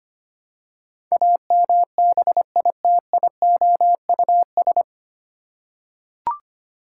Morse code, heard as a keyed tone, about 25 wpm, 700 Hz. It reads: AMBITIOUS
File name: practice / 25wpm / 9 Letter Words - First 3 Letters Given - 25wpm